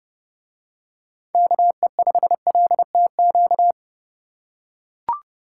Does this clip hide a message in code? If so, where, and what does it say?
Yes: Morse code from 1.3 to 3.7 s, heard as a keyed tone, about 30 wpm, 700 Hz. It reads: KE5LTQ